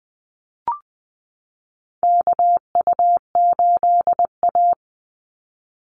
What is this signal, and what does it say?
Morse code, heard as a keyed tone, about 20 wpm, 700 Hz. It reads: KU8A
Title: USA Call Signs 20wpm